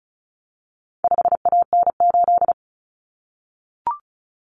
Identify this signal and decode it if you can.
Morse code, heard as a keyed tone, about 34 wpm, 700 Hz. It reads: 5AN8
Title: Common Cut Numbers - 4-Digit 35wpm